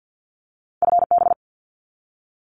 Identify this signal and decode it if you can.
Morse code, heard as a keyed tone, about 48 wpm, 700 Hz. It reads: FB
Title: Abbreviations 50wpm